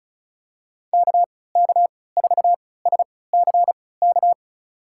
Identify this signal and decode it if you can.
Morse code, heard as a keyed tone, about 34 wpm, spaced as for 20 wpm, 700 Hz. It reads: KK4SCK
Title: USA Call Signs - 3x Character Spacing - 35wpm